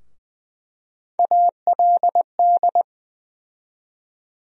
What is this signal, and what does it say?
Morse code, heard as a keyed tone, about 20 wpm, 700 Hz. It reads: ALD
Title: USA Call Sign Postfixes - 3 Letters 20wpm